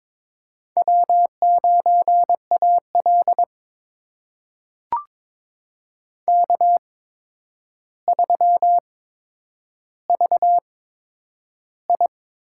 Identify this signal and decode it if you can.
Morse code, heard as a keyed tone, about 21 wpm, spaced as for 9 wpm, 700 Hz. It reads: W9AL K3VI
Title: USA Call Signs - 4 Characters - 8x Character Spacing - 22wpm